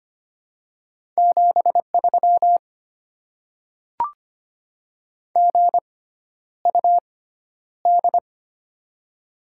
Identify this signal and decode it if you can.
Morse code, heard as a keyed tone, about 25 wpm, spaced as for 11 wpm, 700 Hz. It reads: 73 GUD
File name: Top 20 QSO Elements - 6x Character Spacing - 25wpm